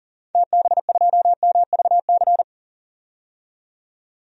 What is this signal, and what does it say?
Morse code, heard as a keyed tone, about 40 wpm, 700 Hz. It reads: TB2MVC
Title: DXCC-100 Call Signs - 6 Characters - 40wpm